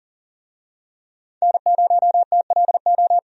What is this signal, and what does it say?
Morse code, heard as a keyed tone, about 40 wpm, 700 Hz. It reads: N0TLO
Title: USA Call Signs - 5 Characters - 40wpm